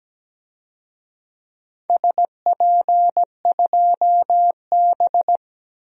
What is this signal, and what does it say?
Morse code, heard as a keyed tone, about 17 wpm, 700 Hz. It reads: SP2B
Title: DXCC-100 Call Signs - 4 Characters - Repeated 2x - 17wpm